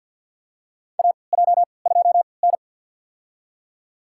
Morse code, heard as a keyed tone, about 50 wpm, spaced as for 28 wpm, 700 Hz. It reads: AJ2N
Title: USA Call Signs - 4 Characters - 3x Character Spacing - 50wpm